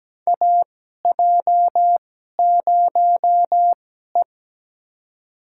Morse code, heard as a keyed tone, about 17 wpm, spaced as for 12 wpm, 700 Hz. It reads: AJ0E
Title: USA Call Signs - 4 Characters - 2x Character Spacing - 17wpm